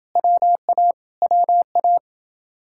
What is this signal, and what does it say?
Morse code, heard as a keyed tone, about 30 wpm, 700 Hz. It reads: WA WA